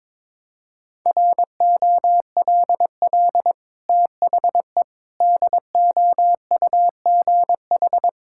Morse code, heard as a keyed tone, about 22 wpm, 700 Hz. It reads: ROLL THE DOUGH
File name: practice / 22wpm / Easy Sentences with Hints - 22wpm